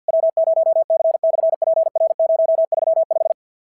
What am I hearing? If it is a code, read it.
Morse code, heard as a keyed tone, about 52 wpm, 690 Hz. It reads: W0C/PR035
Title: SOTA QSO Practice 50wpm